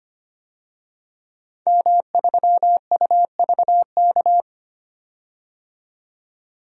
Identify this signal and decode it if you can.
Morse code, heard as a keyed tone, about 25 wpm, 700 Hz. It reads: M3UVK
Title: DXCC-100 Call Signs - 5 Characters - 25wpm